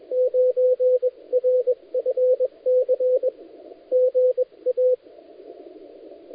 Morse code, heard as a keyed tone, about 23 wpm, spaced as for 20 wpm, 510 Hz, with radio noise. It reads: 9RFC GA
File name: dlazaawansowanych / nagraniazpasma / titawaNETcontrol